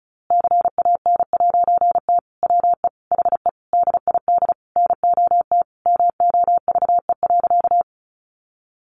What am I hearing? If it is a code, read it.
Morse code, heard as a keyed tone, about 34 wpm, 700 Hz. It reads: CAN'T WE HE DID NOT MOVE.
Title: A Sound Of Thunder 35wpm